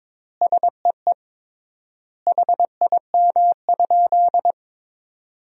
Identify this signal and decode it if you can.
Morse code, heard as a keyed tone, about 21 wpm, 700 Hz. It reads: SEE HIM?
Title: 3x Word-Spacing - Sentences from Top 100 Words - 22wpm